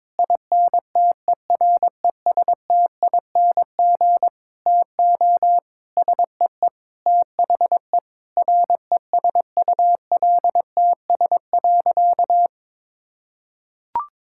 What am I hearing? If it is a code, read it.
Morse code, heard as a keyed tone, about 21 wpm, 700 Hz. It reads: INTERESTING TO SEE THE RESULTS.
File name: Sentences with 7-Word Phrases - 22wpm